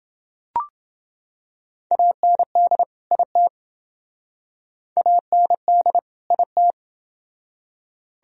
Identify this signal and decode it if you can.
Morse code, heard as a keyed tone, about 28 wpm, 700 Hz. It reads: AND IT AND IT